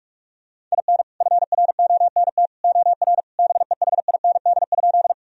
Morse code, heard as a keyed tone, about 46 wpm, 700 Hz. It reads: IN FRONT OR BEHIND?
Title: Sentences from Top 1000 Words - Review Part 2 - 45wpm